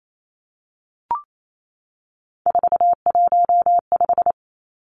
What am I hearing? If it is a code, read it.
Morse code, heard as a keyed tone, about 28 wpm, 700 Hz. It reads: 415
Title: Numbers - 3-Digit 28wpm